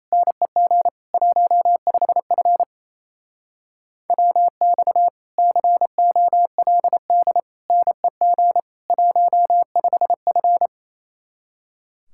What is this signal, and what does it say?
Morse code, heard as a keyed tone, about 36 wpm, 700 Hz. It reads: NEG 15F WX COLD NEG 15F